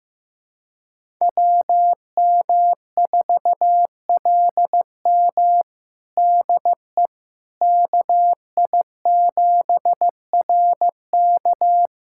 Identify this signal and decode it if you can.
Morse code, heard as a keyed tone, about 15 wpm, 700 Hz. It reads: WM4LM DE KI7RK
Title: USA Callsign DE Callsign - 5 Characters - 15wpm